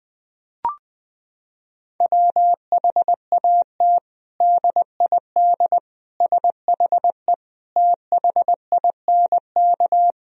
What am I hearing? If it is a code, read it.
Morse code, heard as a keyed tone, about 20 wpm, 700 Hz. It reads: WHAT DID SHE THINK
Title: Sentences from Top 1000 Words - Review Part 4 - 20wpm